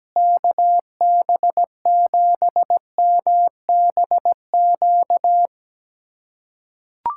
Morse code, heard as a keyed tone, about 17 wpm, 700 Hz. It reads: KB7MBQ
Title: USA Callsign DE Callsign - 6 Characters - 17wpm